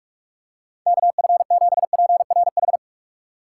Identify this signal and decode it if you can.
Morse code, heard as a keyed tone, about 42 wpm, 700 Hz. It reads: KF7PRH
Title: USA Call Signs - 6 Characters - 4x Character Spacing - 45wpm